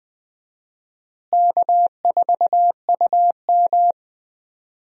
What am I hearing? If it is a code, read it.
Morse code, heard as a keyed tone, about 21 wpm, 700 Hz. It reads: K4UM
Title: USA Call Signs - 4 Characters - Repeated 2x - 20wpm